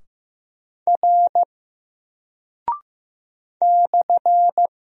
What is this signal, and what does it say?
Morse code, heard as a keyed tone, about 15 wpm, 700 Hz. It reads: R /
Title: Single Character (Letter:Number:Punctuation:Prosign) 15wpm